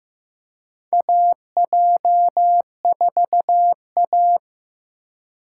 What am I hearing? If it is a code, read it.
Morse code, heard as a keyed tone, about 15 wpm, 700 Hz. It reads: AJ4A
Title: USA Call Signs - 4 Characters - 2x Character Spacing - 15wpm